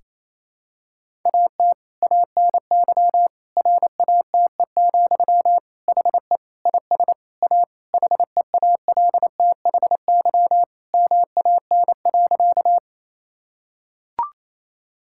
Morse code, heard as a keyed tone, about 28 wpm, 700 Hz. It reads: AT ANY RATE, HE IS A HEALTHY MAN.
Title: Sentences from Top 1000 Words - Review Part 5 - 28wpm